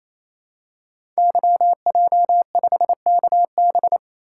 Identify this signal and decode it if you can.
Morse code, heard as a keyed tone, about 29 wpm, 700 Hz. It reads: YJ5KB
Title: Code Groups - 5 Letter:Numbers - 28wpm